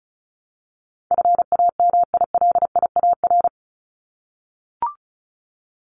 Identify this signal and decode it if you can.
Morse code, heard as a keyed tone, about 34 wpm, 700 Hz. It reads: FAMILIAR
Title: Top 751-1000 Words 35wpm